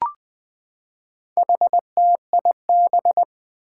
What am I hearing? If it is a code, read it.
Morse code, heard as a keyed tone, about 20 wpm, 700 Hz. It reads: HTIB
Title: Four Challenging Letter-Number Combinations 20wpm